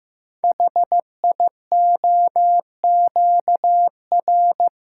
Morse code, heard as a keyed tone, about 15 wpm, 700 Hz. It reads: HIOQR
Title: Code Groups - 5 Letter:Numbers - 15wpm